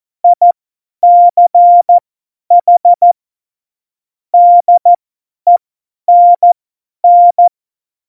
Morse code, heard as a keyed tone, about 14 wpm, spaced as for 10 wpm, 700 Hz. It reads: ICH DENN
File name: Morsekurs / Witze / Witz 13. Oktober 2020, 50 F70